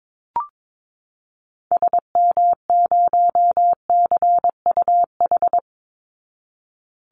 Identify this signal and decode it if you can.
Morse code, heard as a keyed tone, about 22 wpm, 700 Hz. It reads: SM0CUH
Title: Call Signs Repeated 2x - 22wpm